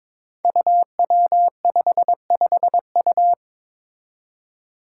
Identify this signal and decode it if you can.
Morse code, heard as a keyed tone, about 22 wpm, 700 Hz. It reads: UW55U
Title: Five Challenging Letter-Number Combinations 22wpm